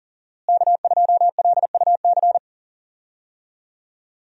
Morse code, heard as a keyed tone, about 40 wpm, 700 Hz. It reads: K2LUC
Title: USA Call Signs - 5 Characters - Repeated 3x - 40wpm